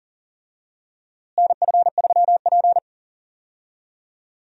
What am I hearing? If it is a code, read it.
Morse code, heard as a keyed tone, about 40 wpm, 700 Hz. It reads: NF3P
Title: USA Call Signs - 4 Characters - Repeated 2x - 40wpm